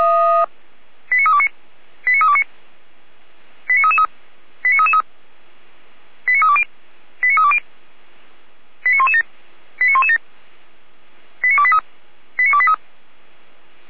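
Five-tone selective call (ZVEI), heard as quick runs of five tones, tones of about 70 ms. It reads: 89329 89329 89333 89333 89320 89320 89117 89117 89283 89283
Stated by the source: ZVEI five-tone call received by SDR.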